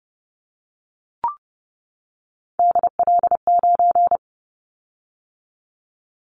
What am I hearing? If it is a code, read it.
Morse code, heard as a keyed tone, about 31 wpm, 700 Hz. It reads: DL9
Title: Callsign Prefixes for Top 50 DXCC - 30wpm